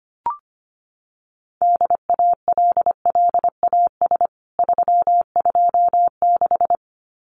Morse code, heard as a keyed tone, about 25 wpm, 700 Hz. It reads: DALLAS 326
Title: CWT Exchange 25wpm